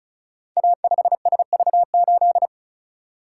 Morse code, heard as a keyed tone, about 36 wpm, 690 Hz. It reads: A5SV8
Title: Code Groups - 5 Letter:Numbers - 35wpm